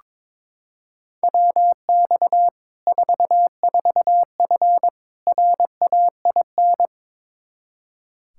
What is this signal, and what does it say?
Morse code, heard as a keyed tone, about 22 wpm, 700 Hz. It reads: WX 44F RAIN
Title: Practice QSO - Exchange 2 - Info - 22wpm